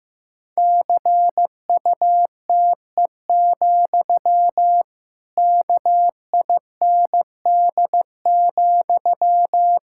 Morse code, heard as a keyed tone, about 15 wpm, 700 Hz. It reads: CUTE, KIND,
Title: Pangram Challenge v2 - 15wpm